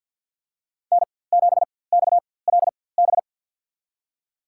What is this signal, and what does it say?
Morse code, heard as a keyed tone, about 47 wpm, spaced as for 23 wpm, 700 Hz. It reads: N7XLB
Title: USA Call Signs - 4x Character Spacing - 50wpm